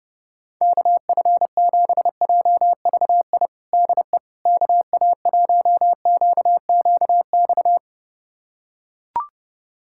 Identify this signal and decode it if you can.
Morse code, heard as a keyed tone, about 29 wpm, 700 Hz. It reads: KF7JVI DE KA1QQX
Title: USA Callsign DE Callsign - 6 Characters - 30wpm